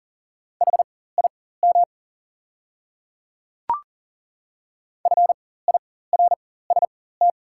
Morse code, heard as a keyed tone, about 39 wpm, spaced as for 18 wpm, 690 Hz. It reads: HIM FIRST
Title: 4x Character Spacing - Top 100 Words - 40wpm